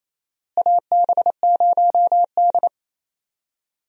Morse code, heard as a keyed tone, about 28 wpm, 700 Hz. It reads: AB0D
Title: USA Call Signs - 4 Characters - Repeated 2x - 28wpm